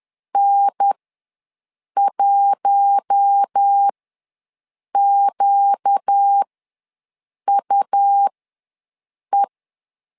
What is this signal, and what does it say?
Morse code, heard as a keyed tone, about 10 wpm, spaced as for 6 wpm, 800 Hz. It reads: N1QUE
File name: LEZIONI / 4 Q1 ws 10wpm 2538